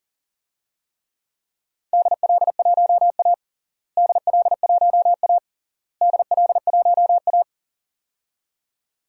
Morse code, heard as a keyed tone, about 40 wpm, 700 Hz. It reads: DL1A DL1A DL1A
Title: DXCC-100 Call Signs - 4 Characters - Repeated 3x - 40wpm